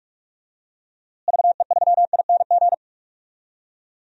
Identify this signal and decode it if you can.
Morse code, heard as a keyed tone, about 44 wpm, 700 Hz. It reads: VE3ING
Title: DXCC-100 Call Signs - 6 Characters - Repeated 3x - 45wpm